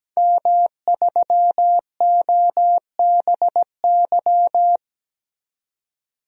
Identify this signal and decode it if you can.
Morse code, heard as a keyed tone, about 17 wpm, 700 Hz. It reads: M3OBY
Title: DXCC-100 Call Signs - 5 Characters - 17wpm